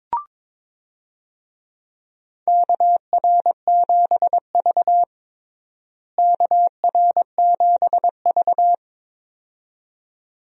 Morse code, heard as a keyed tone, about 22 wpm, 700 Hz. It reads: KR7V KR7V
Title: USA Call Signs - 4 Characters - Repeated 2x - 22wpm